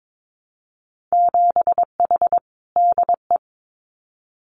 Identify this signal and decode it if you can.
Morse code, heard as a keyed tone, about 22 wpm, 700 Hz. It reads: 7H DE
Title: Winter Field Day Exchanges Encore 22wpm